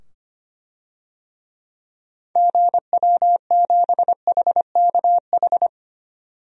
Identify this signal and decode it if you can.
Morse code, heard as a keyed tone, about 25 wpm, 700 Hz. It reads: GW7HKH
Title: DXCC-100 Call Signs - 6 Characters - 25wpm